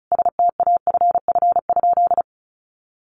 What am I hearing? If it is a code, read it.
Morse code, heard as a keyed tone, about 34 wpm, 700 Hz. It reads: STAFF?
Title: Sentences from Top 1200 Words 35wpm